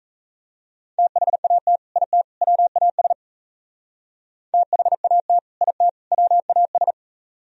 Morse code, heard as a keyed tone, about 36 wpm, 700 Hz. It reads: THAT IT WAS THAT IT WAS